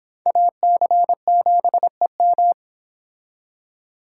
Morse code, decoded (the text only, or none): AC7EM